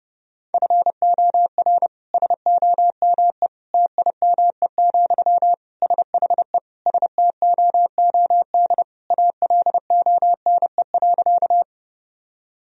Morse code, heard as a keyed tone, about 30 wpm, 700 Hz. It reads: FOR SOME TIME, SHE STOOD ALONE.
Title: Sentences from Top 1000 Words - Review Part 2 - 30wpm